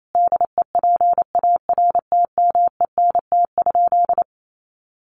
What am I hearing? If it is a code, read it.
Morse code, heard as a keyed tone, about 28 wpm, 700 Hz. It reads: DEPARTMENT?
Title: Sentences from Top 1200 Words 28wpm